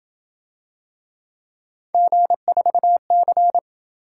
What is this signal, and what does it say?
Morse code, heard as a keyed tone, about 26 wpm, 700 Hz. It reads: G4C